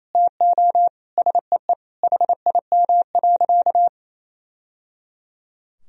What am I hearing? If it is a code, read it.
Morse code, heard as a keyed tone, about 27 wpm, 700 Hz. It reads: TO SEE HIM.
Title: Sentences from Top 1000 Words - Review Part 2 - 28wpm